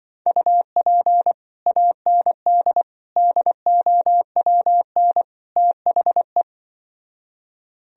Morse code, heard as a keyed tone, about 24 wpm, 700 Hz. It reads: UP AND DOWN THE
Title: Speed-Racing - Sets of 4 Words 20wpm